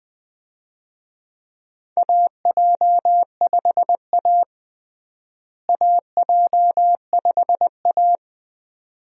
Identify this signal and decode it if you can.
Morse code, heard as a keyed tone, about 20 wpm, 700 Hz. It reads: AJ5A AJ5A
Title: USA Call Signs - 4 Characters - Repeated 2x - 20wpm